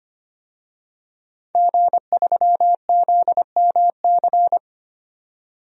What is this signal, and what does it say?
Morse code, heard as a keyed tone, about 25 wpm, 700 Hz. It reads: G3ZMC
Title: DXCC-100 Call Signs - 5 Characters - 25wpm